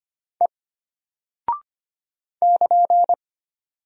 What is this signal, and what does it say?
Morse code, heard as a keyed tone, about 25 wpm, 700 Hz. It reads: E <KN>
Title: Single Character (Letter:Number:Punctuation:Prosign) Alt 25wpm